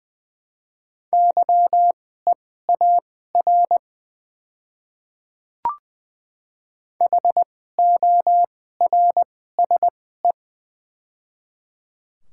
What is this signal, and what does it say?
Morse code, heard as a keyed tone, about 21 wpm, spaced as for 15 wpm, 700 Hz. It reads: YEAR HORSE
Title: 2x Character Spacing - Top 300 Words - 20wpm